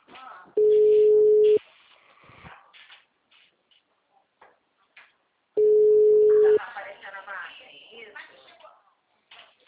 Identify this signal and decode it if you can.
Ringback tone (Europe, 425 Hz), with speech.
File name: out- -2016- - - .46391